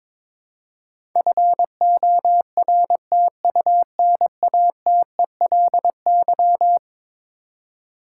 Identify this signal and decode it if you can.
Morse code, heard as a keyed tone, about 22 wpm, 700 Hz. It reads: FORTUNATELY
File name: practice / 22wpm / Eleven Letter Words - 22wpm